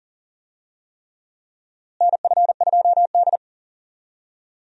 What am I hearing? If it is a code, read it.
Morse code, heard as a keyed tone, about 40 wpm, 700 Hz. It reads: NF2D
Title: USA Call Signs - 4 Characters - Repeated 3x - 40wpm